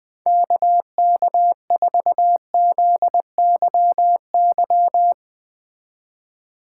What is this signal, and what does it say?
Morse code, heard as a keyed tone, about 20 wpm, 700 Hz. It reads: KK4ZYY